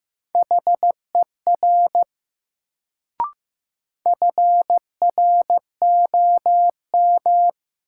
Morse code, heard as a keyed tone, about 15 wpm, 700 Hz. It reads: HER FROM
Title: Sentences from Top 100 Words - New Word Drill - 15wpm